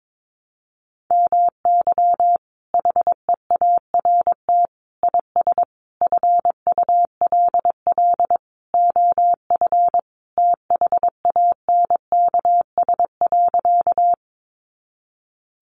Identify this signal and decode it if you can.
Morse code, heard as a keyed tone, about 22 wpm, 700 Hz. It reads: MY HEART IS FULL OF THANKS.
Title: Sentences from Top 1200 Words 22wpm